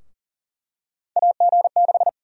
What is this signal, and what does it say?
Morse code, heard as a keyed tone, about 40 wpm, 700 Hz. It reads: AG6
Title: USA Call Sign Prefixes - 40wpm